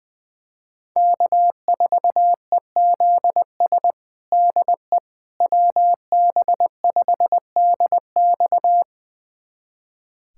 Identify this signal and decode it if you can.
Morse code, heard as a keyed tone, about 20 wpm, 700 Hz. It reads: K4EZS DE WB5DX
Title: USA Callsign DE Callsign - 5 Characters - 20wpm